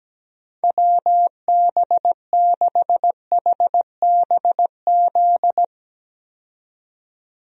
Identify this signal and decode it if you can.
Morse code, heard as a keyed tone, about 17 wpm, 700 Hz. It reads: WB6HBZ
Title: USA Call Signs - 6 Characters - 17wpm